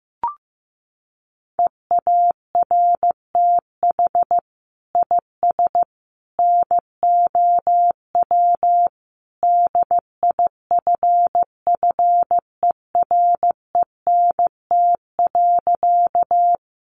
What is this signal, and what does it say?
Morse code, heard as a keyed tone, about 15 wpm, 700 Hz. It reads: EARTH IS NOW DIFFERENT.
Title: Sentences from Top 600 Words 15wpm